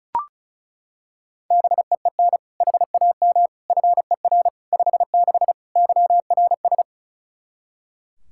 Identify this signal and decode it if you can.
Morse code, heard as a keyed tone, about 34 wpm, 700 Hz. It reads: BEEN HAM FER 56 YRS
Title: Practice QSO - Exchange 3 - More Info - 35wpm